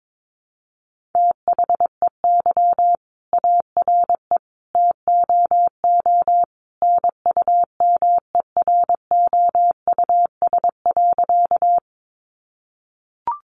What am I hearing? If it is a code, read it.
Morse code, heard as a keyed tone, about 22 wpm, 700 Hz. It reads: THEY ARE TOO NUMEROUS.
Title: Sentences from Top 1400 Words 22wpm